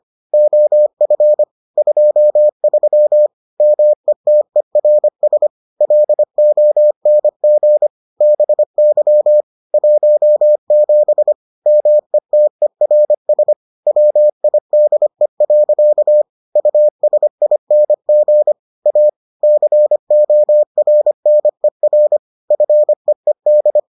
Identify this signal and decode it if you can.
Morse code, heard as a keyed tone, about 25 wpm, 600 Hz. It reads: OF 23 METERS LONG BY 17 METERS WIDE. USING A CORNER FEED